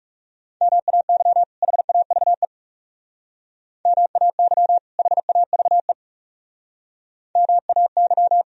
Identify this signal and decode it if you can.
Morse code, heard as a keyed tone, about 36 wpm, 700 Hz. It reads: MAY HAVE MAY HAVE MAY
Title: Speed-Racing - Sets of 2 Words 30wpm